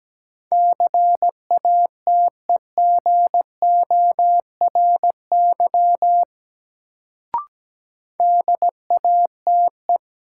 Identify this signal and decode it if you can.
Morse code, heard as a keyed tone, about 17 wpm, 700 Hz. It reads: CATEGORY DATE
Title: Sentences from Top 1000 Words - Review Part 7 - New Word Drill - 17wpm